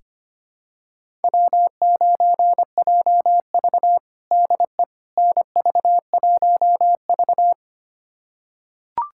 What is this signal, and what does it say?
Morse code, heard as a keyed tone, about 25 wpm, 700 Hz. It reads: W9JV DE NV1V
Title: USA Callsign DE Callsign - 4 Characters - 25wpm